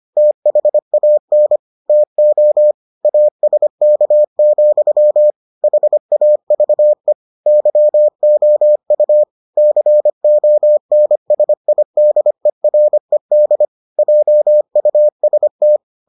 Morse code, heard as a keyed tone, about 25 wpm, 600 Hz. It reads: THAN TO ASK, HAVE YOU CONSIDERED JUST